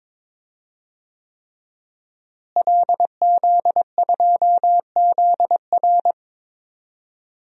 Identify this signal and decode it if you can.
Morse code, heard as a keyed tone, about 22 wpm, 700 Hz. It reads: LZ2ZR
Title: DXCC-100 Call Signs - 5 Characters - Repeated 3x - 22wpm